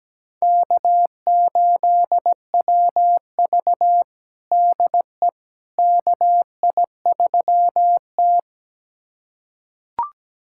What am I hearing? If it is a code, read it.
Morse code, heard as a keyed tone, about 17 wpm, 700 Hz. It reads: K8WV DE KI3T
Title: USA Callsign DE Callsign - 4 Characters - 17wpm